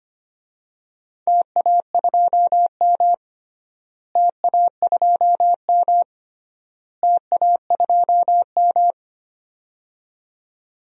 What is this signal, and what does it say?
Morse code, heard as a keyed tone, about 25 wpm, 700 Hz. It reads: TA2M TA2M TA2M
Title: DXCC-100 Call Signs - 4 Characters - Repeated 3x - 25wpm